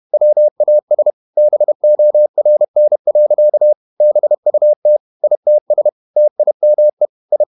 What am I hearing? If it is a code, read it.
Morse code, heard as a keyed tone, about 31 wpm, 600 Hz. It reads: WAS BORN. BUT ITS TIME I
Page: Never 31wpm | CW med Gnister